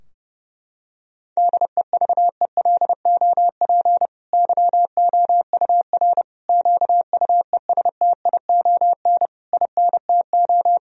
Morse code, heard as a keyed tone, about 32 wpm, 700 Hz. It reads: DEVELOP YOUR QUESTION INTO
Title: Sentences from Top 1000 Words - Review Part 2 - 30wpm